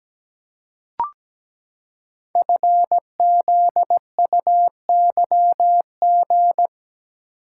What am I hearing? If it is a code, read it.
Morse code, heard as a keyed tone, about 17 wpm, 700 Hz. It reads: FZUYG
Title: Code Groups - 5 Letter:Numbers - 17wpm